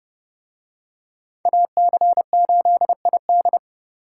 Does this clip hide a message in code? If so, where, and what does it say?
Yes: Morse code from 1.5 to 3.6 s, heard as a keyed tone, about 30 wpm, 700 Hz. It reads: AC8ID